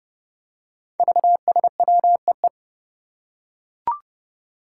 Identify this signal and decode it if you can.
Morse code, heard as a keyed tone, about 30 wpm, 700 Hz. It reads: VSWEE
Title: Five Challenging Letter-Number Combinations 30wpm